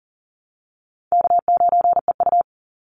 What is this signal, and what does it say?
Morse code, heard as a keyed tone, about 37 wpm, 700 Hz. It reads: K9EU
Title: Call Signs - 40wpm